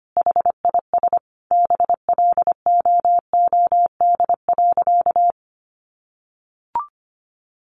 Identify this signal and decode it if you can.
Morse code, heard as a keyed tone, about 25 wpm, 700 Hz. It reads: HIS BLOOD.
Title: Sentences from Top 1400 Words 25wpm